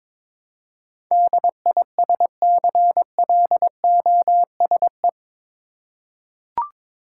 Morse code, heard as a keyed tone, about 22 wpm, 700 Hz. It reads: DISCLOSE
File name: practice / 22wpm / Eight Letter Words - 22wpm